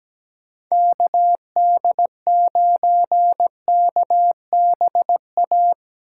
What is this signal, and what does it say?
Morse code, heard as a keyed tone, about 17 wpm, 700 Hz. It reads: KD9KBA
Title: USA Call Signs - 8x Character Spacing - 17wpm